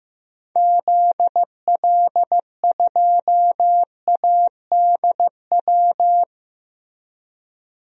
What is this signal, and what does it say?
Morse code, heard as a keyed tone, about 15 wpm, 700 Hz. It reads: ZL2ADW
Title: DXCC-100 Call Signs - 6 Characters - Repeated 3x - 15wpm